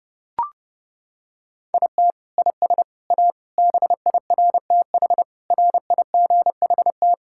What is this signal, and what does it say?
Morse code, heard as a keyed tone, about 30 wpm, 700 Hz. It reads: IT IS A BIRTH RIGHT
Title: Sentences from Top 1000 Words - Review Part 5 - 30wpm